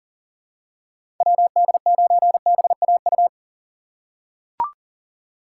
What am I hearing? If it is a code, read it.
Morse code, heard as a keyed tone, about 40 wpm, 700 Hz. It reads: WD9BAU
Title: USA Call Signs - 6 Characters - 6x Character Spacing - 40wpm